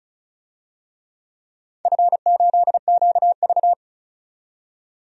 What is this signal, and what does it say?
Morse code, heard as a keyed tone, about 36 wpm, 700 Hz. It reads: F8QV
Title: DXCC-100 Call Signs - 4 Characters - 35wpm